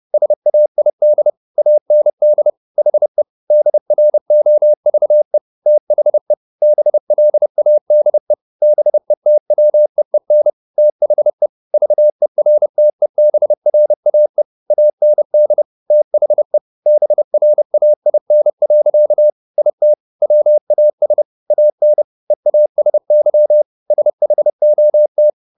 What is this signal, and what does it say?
Morse code, heard as a keyed tone, about 30 wpm, 600 Hz. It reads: SAID AND HE DROVE THE BLADE BETWEEN THE VERTEBRAE AND THE BRAIN. IT WAS AN EASY SHOT